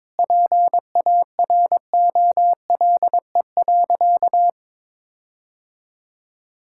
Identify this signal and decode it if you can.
Morse code, heard as a keyed tone, about 22 wpm, 700 Hz. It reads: PAROLE.
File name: practice / 22wpm / Sentences with 8-Word Phrases - 22wpm